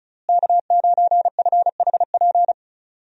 Morse code, heard as a keyed tone, about 36 wpm, 700 Hz. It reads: K9FHP
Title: USA Call Signs - 5 Characters - Repeated 3x - 35wpm